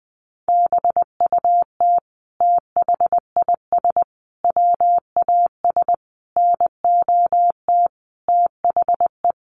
Morse code, heard as a keyed tone, about 20 wpm, 700 Hz. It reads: BUT THIS WAS NOT THE
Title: Sentences from Top 700 Words 20wpm